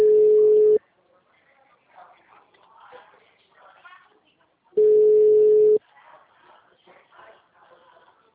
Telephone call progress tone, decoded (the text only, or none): ringback tone